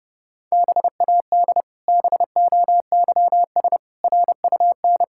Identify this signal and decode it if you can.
Morse code, heard as a keyed tone, about 30 wpm, 700 Hz. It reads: BAD BOYS RUN
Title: Pangram Challenge v2 - 30wpm